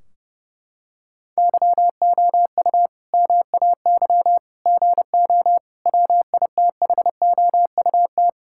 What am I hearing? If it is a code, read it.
Morse code, heard as a keyed tone, about 30 wpm, 700 Hz. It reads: YOU MAY GO WITHOUT
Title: Sentences from Top 1600 Words 30wpm